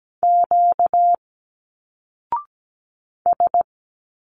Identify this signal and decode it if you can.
Morse code, heard as a keyed tone, about 17 wpm, 700 Hz. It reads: Q S
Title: Single Letters 17wpm